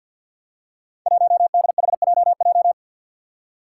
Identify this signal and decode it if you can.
Morse code, heard as a keyed tone, about 52 wpm, 700 Hz. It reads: 1DHJJ
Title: Five Challenging Letter-Number Combinations 50wpm